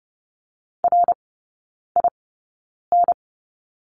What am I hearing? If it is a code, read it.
Morse code, heard as a keyed tone, about 30 wpm, spaced as for 9 wpm, 700 Hz. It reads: RIN
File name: course / 30:10wpm / 087 Lesson 15e - Number 5 - Three Characters 30wpm